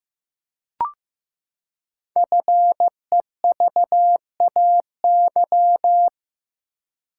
Morse code, heard as a keyed tone, about 15 wpm, 700 Hz. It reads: FEVAY
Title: Code Groups - 5 Letter:Numbers - 15wpm